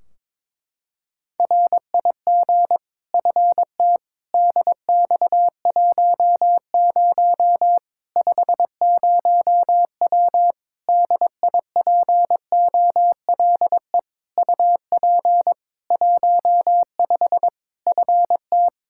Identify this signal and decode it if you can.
Morse code, heard as a keyed tone, about 21 wpm, 700 Hz. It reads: RIG FT DX10 50W DIPOLE UP 15 FT
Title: K1USN SST QSO Practice 22wpm